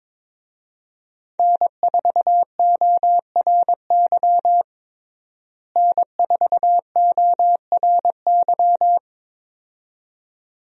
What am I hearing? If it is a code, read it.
Morse code, heard as a keyed tone, about 21 wpm, 700 Hz. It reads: N4ORY N4ORY
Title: USA Call Signs - 5 Characters - Repeated 2x - 22wpm